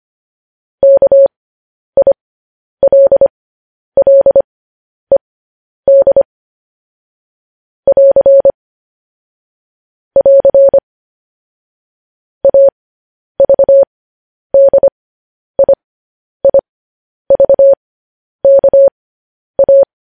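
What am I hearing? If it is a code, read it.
Morse code, heard as a keyed tone, about 25 wpm, spaced as for 10 wpm, 550 Hz. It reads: KILLED <AR> <AR> AVDIIVKA
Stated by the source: Morse code news headlines.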